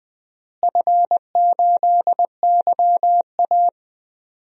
Morse code, heard as a keyed tone, about 20 wpm, 700 Hz. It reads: F8YA
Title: DXCC-100 Call Signs - 4 Characters - Repeated 2x - 20wpm